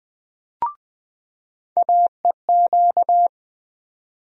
Morse code, heard as a keyed tone, about 20 wpm, 700 Hz. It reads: AEQ
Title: USA Call Sign Postfixes - 3 Letters 20wpm